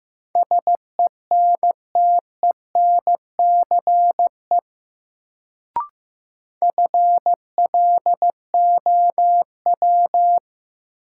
Morse code, heard as a keyed tone, about 15 wpm, 700 Hz. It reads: SENTENCE FLOW
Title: Sentences from Top 1200 Words - New Word Drill Alt - 15wpm